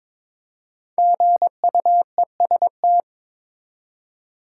Morse code, heard as a keyed tone, about 23 wpm, 700 Hz. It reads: GUEST
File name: 5 Letter Words - First 3 Letters Given - 22wpm